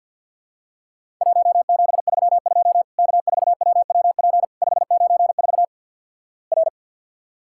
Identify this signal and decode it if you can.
Morse code, heard as a keyed tone, about 50 wpm, 700 Hz. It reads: 1732 K4WWP 594 R
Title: Worked All Europe DX Contest - QTC Exchange - 50wpm